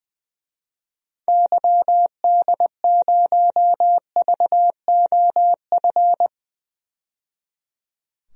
Morse code, heard as a keyed tone, about 20 wpm, 700 Hz. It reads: YD0VOF
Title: DXCC-100 Call Signs - 6 Characters - 20wpm